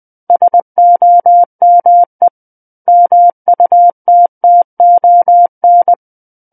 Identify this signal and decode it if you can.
Morse code, heard as a keyed tone, about 20 wpm, 700 Hz. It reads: SOME MUTTON
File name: 20 WPM